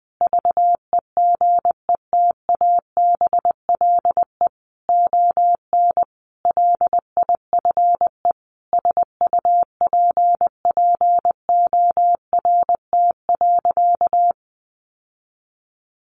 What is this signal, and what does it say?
Morse code, heard as a keyed tone, about 20 wpm, 700 Hz. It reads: VEGETABLE ON LIFE SUPPORT.
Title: Sentences from Top 1300 Words 20wpm